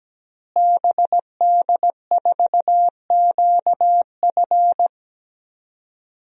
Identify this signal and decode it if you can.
Morse code, heard as a keyed tone, about 17 wpm, 700 Hz. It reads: BD4QF
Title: DXCC-100 Call Signs - 5 Characters - Repeated 3x - 17wpm